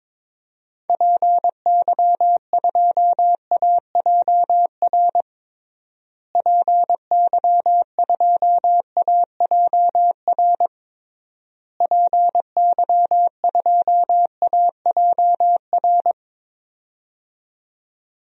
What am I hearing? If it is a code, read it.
Morse code, heard as a keyed tone, about 21 wpm, 700 Hz. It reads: PY2AJR PY2AJR PY2AJR
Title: DXCC-100 Call Signs - 6 Characters - Repeated 3x - 22wpm